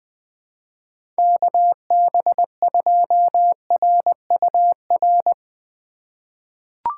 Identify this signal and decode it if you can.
Morse code, heard as a keyed tone, about 20 wpm, 700 Hz. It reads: KB2RUR
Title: USA Call Signs - 6 Characters - 2x Character Spacing - 20wpm